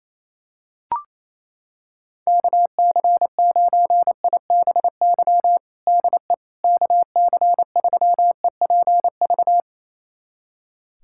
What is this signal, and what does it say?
Morse code, heard as a keyed tone, about 28 wpm, 700 Hz. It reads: KC9IBY DE KC3EPV
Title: USA Callsign DE Callsign - 6 Characters - 28wpm